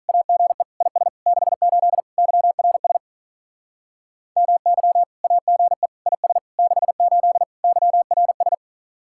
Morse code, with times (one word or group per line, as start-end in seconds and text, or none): 0.1-0.6 AGE
0.8-1.1 IS
1.3-2.0 68
2.2-3.0 YRS
4.4-5.0 MY
5.2-5.9 AGE
6.1-6.4 IS
6.6-7.4 68
7.6-8.6 YRS